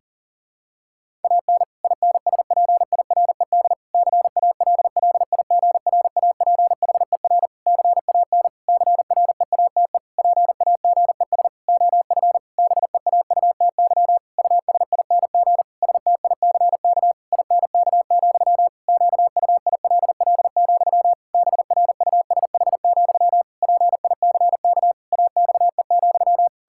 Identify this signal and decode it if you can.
Morse code, heard as a keyed tone, about 40 wpm, 700 Hz. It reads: AN INSPIRED CALLIGRAPHER CAN CREATE PAGES OF BEAUTY USING STICK INK, QUILL, BRUSH, PICK AXE,
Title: Pangram Challenge v2 - 40wpm